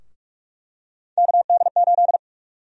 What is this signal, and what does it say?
Morse code, heard as a keyed tone, about 41 wpm, 700 Hz. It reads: KD8
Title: USA Call Sign Prefixes - Repeated 2x - 45wpm